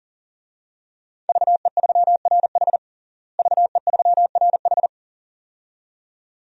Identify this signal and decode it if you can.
Morse code, heard as a keyed tone, about 40 wpm, 700 Hz. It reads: VE3RH VE3RH
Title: DXCC-100 Call Signs - 5 Characters - Repeated 2x - 40wpm